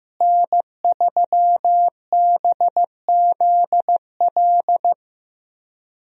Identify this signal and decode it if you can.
Morse code, heard as a keyed tone, about 15 wpm, 700 Hz. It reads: N3BZL
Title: USA Call Signs - 5 Characters - Repeated 3x - 15wpm